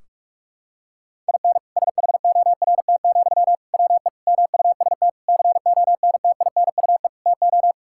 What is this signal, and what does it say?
Morse code, heard as a keyed tone, about 44 wpm, 700 Hz. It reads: IN SHORT, WE MUST CONTINUE TO
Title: Sentences from Top 1600 Words 45wpm